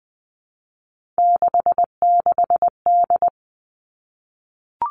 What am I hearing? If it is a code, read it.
Morse code, heard as a keyed tone, about 20 wpm, 700 Hz. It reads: 66D
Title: Three Challenging Letter-Number Combinations 20wpm